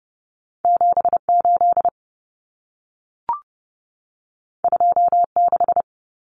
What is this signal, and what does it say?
Morse code, heard as a keyed tone, about 30 wpm, 700 Hz. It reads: 78 26
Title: Common Cut Numbers - 2-Digit 30wpm